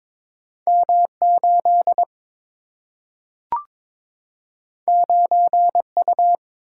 Morse code, heard as a keyed tone, about 22 wpm, 700 Hz. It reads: M 8 9 U
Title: Single-Letter Single-Number 22wpm